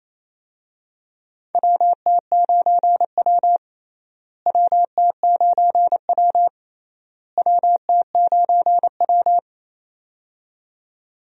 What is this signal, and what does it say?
Morse code, heard as a keyed tone, about 28 wpm, 700 Hz. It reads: WT9W WT9W WT9W
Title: USA Call Signs - 4 Characters - Repeated 3x - 28wpm